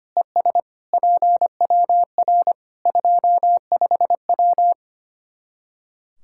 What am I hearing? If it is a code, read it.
Morse code, heard as a keyed tone, about 25 wpm, 700 Hz. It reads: ES PWR 25W
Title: K1USN SST QSO Practice 25wpm